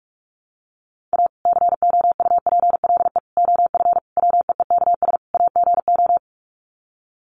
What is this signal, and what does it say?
Morse code, heard as a keyed tone, about 44 wpm, 700 Hz. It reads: A COUPLE OF WEEKS AGO
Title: Sets of 5 Words 45wpm